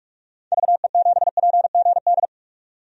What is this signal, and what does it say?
Morse code, heard as a keyed tone, about 47 wpm, 700 Hz. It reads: VE7PGD
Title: DXCC-100 Call Signs - 6 Characters - 45wpm